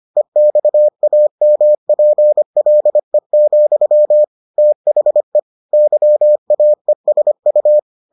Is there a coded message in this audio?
Yes: Morse code from 0.2 to 7.8 s, heard as a keyed tone, about 25 wpm, 600 Hz. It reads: EXAMPLE, THE YAESU